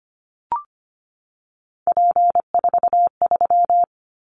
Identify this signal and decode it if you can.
Morse code, heard as a keyed tone, about 25 wpm, 700 Hz. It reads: P43
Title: Callsign Prefixes for Top 100 DXCC - 25wpm